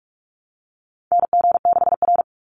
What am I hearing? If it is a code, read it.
Morse code, heard as a keyed tone, about 44 wpm, 700 Hz. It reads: NG6R
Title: NA SOTA Chaser Callsigns 45wpm